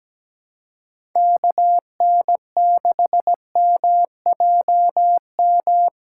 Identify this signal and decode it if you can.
Morse code, heard as a keyed tone, about 17 wpm, 700 Hz. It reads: KN6MJM
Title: USA Callsign DE Callsign - 6 Characters - 17wpm